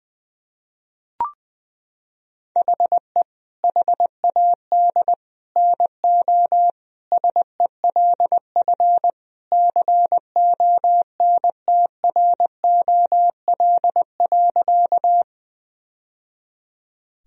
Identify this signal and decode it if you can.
Morse code, heard as a keyed tone, about 20 wpm, 700 Hz. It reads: HE HAD NO SELF CONTROL.
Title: Sentences from Top 1000 Words - Review Part 3 - 20wpm